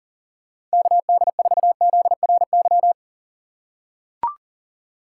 Morse code, heard as a keyed tone, about 40 wpm, 700 Hz. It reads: KD4ZRY
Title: USA Call Signs - 6 Characters - 4x Character Spacing - 40wpm